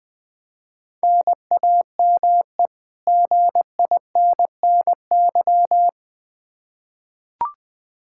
Morse code, heard as a keyed tone, about 20 wpm, 700 Hz. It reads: NAME GINNY
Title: Practice QSO - Exchange 1 - Beginning - 20wpm